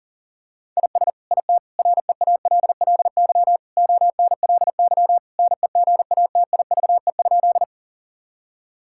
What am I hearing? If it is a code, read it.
Morse code, heard as a keyed tone, about 37 wpm, 700 Hz. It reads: IS IT REALLY ONLY NEGATIVE?
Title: Sentences from Top 1000 Words - Review Part 6 - 40wpm